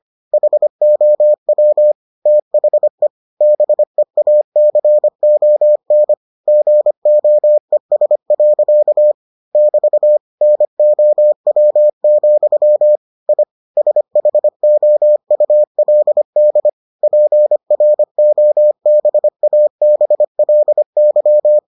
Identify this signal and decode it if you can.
Morse code, heard as a keyed tone, about 25 wpm, 600 Hz. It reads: HOW THE BEACON GOES. <BT> NOW, I SHOULD PROBABLY